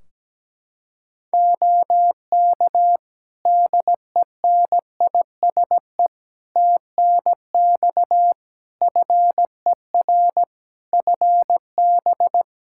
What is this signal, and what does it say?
Morse code, heard as a keyed tone, about 17 wpm, 700 Hz. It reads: OK DENISE TNX FER FB
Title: Practice QSO - Exchange 4 - Ending - 17wpm